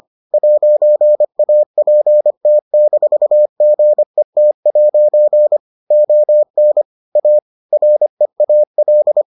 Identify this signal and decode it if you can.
Morse code, heard as a keyed tone, about 25 wpm, 600 Hz. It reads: 'APT-GET' ON A REAL